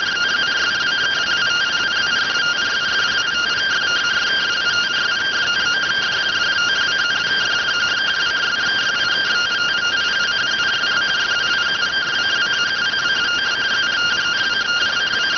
Two-tone teleprinter signal, about 170 Hz shift.